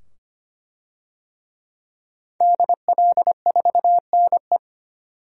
Morse code, heard as a keyed tone, about 25 wpm, 700 Hz. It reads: DL4NE
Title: DXCC-100 Call Signs - 5 Characters - 25wpm